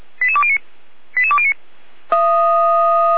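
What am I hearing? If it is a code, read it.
Five-tone selective call (ZVEI), heard as quick runs of five tones, tones of about 70 ms. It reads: 88209 88209
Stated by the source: ZVEI five-tone call received by SDR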